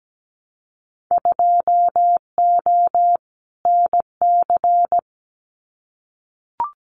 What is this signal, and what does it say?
Morse code, heard as a keyed tone, about 17 wpm, 700 Hz. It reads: 2O NC
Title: Winter Field Day Exchanges 17wpm